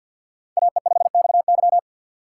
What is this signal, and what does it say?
Morse code, heard as a keyed tone, about 53 wpm, 700 Hz. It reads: AE5XY
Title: USA Call Signs - 5 Characters - 3x Character Spacing - 50wpm